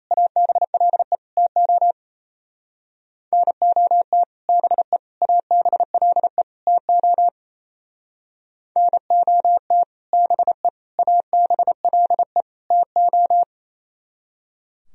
Morse code, heard as a keyed tone, about 35 wpm, 700 Hz. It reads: ABLE TO NOT BE ABLE TO NOT BE ABLE TO
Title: Speed-Racing - Sets of 4 Words 28wpm